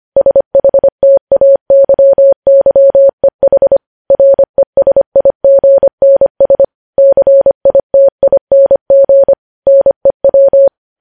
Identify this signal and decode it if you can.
Morse code, heard as a keyed tone, about 25 wpm, 550 Hz. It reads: SHTAYYEH RESIGNS CITING NEW
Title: News Headlines at 25